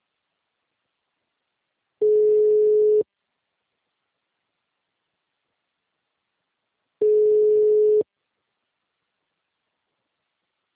Ringback tone (Europe, 425 Hz).
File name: out- -2030- - - .4